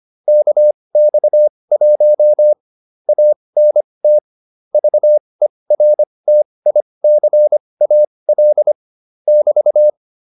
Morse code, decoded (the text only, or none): KX1 ANT VERTICAL <BT>